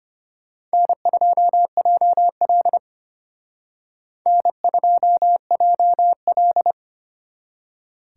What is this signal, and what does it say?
Morse code, heard as a keyed tone, about 31 wpm, 700 Hz. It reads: N2JL N2JL
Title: Speed Racing - USA 1x2 Call Signs 25wpm